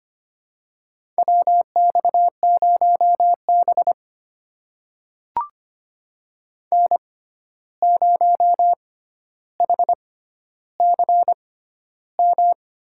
Morse code, heard as a keyed tone, about 25 wpm, spaced as for 12 wpm, 700 Hz. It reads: WX0B N0HCM
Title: USA Call Signs - 6x Character Spacing - 25wpm